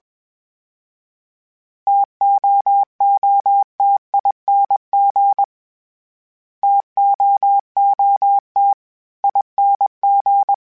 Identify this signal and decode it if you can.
Morse code, heard as a keyed tone, about 22 wpm, 800 Hz. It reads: TOOTING TOOT ING
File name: Variable Speed / Variable Speed-mp3s / TO sp @20